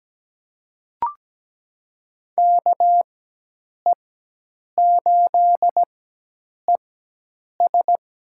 Morse code, heard as a keyed tone, about 17 wpm, spaced as for 8 wpm, 700 Hz. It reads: KE8ES
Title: USA Call Signs - 4x Character Spacing - 17wpm